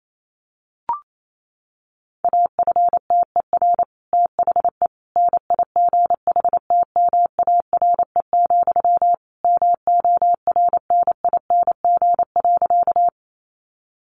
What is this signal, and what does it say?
Morse code, heard as a keyed tone, about 28 wpm, 700 Hz. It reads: AFTER THE NIGHTMARE, MORNING.
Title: A Sound Of Thunder 28wpm